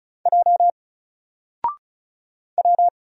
Morse code, heard as a keyed tone, about 36 wpm, 700 Hz. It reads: J W
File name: Single-Letter-Number-35wpm